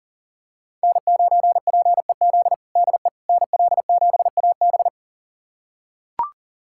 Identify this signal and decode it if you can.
Morse code, heard as a keyed tone, about 40 wpm, 700 Hz. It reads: N9PEZ DE NL7AB
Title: USA Callsign DE Callsign - 5 Characters - 40wpm